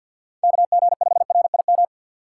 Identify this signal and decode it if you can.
Morse code, heard as a keyed tone, about 52 wpm, 700 Hz. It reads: KG5RIM
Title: USA Call Signs - 6 Characters - 6x Character Spacing - 50wpm